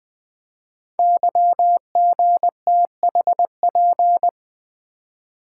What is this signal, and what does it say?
Morse code, heard as a keyed tone, about 20 wpm, 700 Hz. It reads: YGTHP
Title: Code Groups - 5 Letters - 20wpm